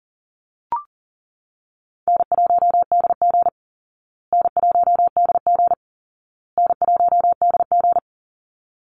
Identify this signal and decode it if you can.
Morse code, heard as a keyed tone, about 37 wpm, 700 Hz. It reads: N1DG N1DG N1DG
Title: Easy Call Signs Repeated 3x - 40wpm